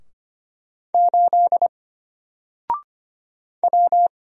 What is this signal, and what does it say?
Morse code, heard as a keyed tone, about 25 wpm, 700 Hz. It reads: 8 W
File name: Single-Letter-Number-25wpm